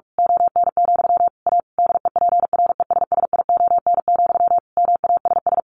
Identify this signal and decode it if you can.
Morse code, heard as a keyed tone, about 44 wpm, 700 Hz. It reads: ON, A DEPRESSION, MASS
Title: A Sound Of Thunder 45wpm